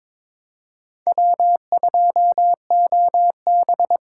Morse code, heard as a keyed tone, about 22 wpm, 700 Hz. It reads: W2OB
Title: USA Call Signs - 4 Characters - 4x Character Spacing - 22wpm